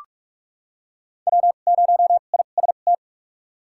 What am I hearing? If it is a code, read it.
Morse code, heard as a keyed tone, about 46 wpm, spaced as for 32 wpm, 700 Hz. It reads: W0IST
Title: USA Call Signs - 5 Characters - 2x Character Spacing - 45wpm